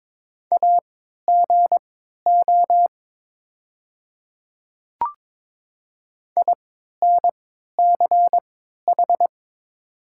Morse code, heard as a keyed tone, about 22 wpm, spaced as for 12 wpm, 700 Hz. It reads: AGO INCH
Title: 3x Character Spacing - Top 500 Words - 22wpm